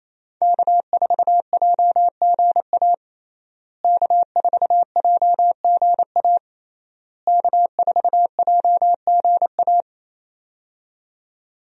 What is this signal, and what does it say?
Morse code, heard as a keyed tone, about 27 wpm, 700 Hz. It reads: K4JGA K4JGA K4JGA
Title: USA Call Signs - 5 Characters - Repeated 3x - 28wpm